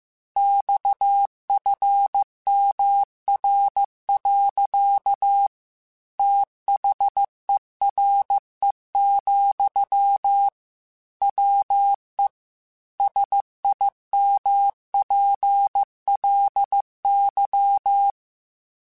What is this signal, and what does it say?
Morse code, heard as a keyed tone, about 15 wpm, 800 Hz. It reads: XFMR. THERE, WE SIMPLY
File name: Technical Subjects / Technical Subjects-mp3s / Sw Pwr Sup @13